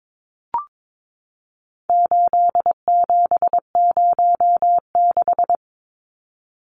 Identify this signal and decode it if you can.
Morse code, heard as a keyed tone, about 22 wpm, 700 Hz. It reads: 8706
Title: Numbers - 4-Digit 22wpm